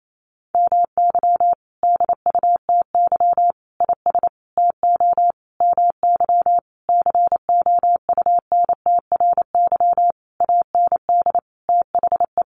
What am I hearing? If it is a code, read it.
Morse code, heard as a keyed tone, about 28 wpm, 700 Hz. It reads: MY DUTY IS TO MY COUNTRY AND THE
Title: Sentences from Top 1500 Words 28wpm